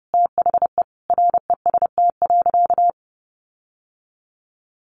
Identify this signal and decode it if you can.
Morse code, heard as a keyed tone, about 30 wpm, 700 Hz. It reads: THE REST.
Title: Sentences from Top 1300 Words 30wpm